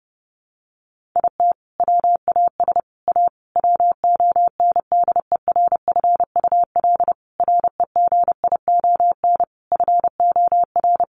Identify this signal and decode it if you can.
Morse code, heard as a keyed tone, about 30 wpm, 700 Hz. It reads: IT WAS A WONDERFUL REGION FOR
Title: Sentences from Top 1500 Words 30wpm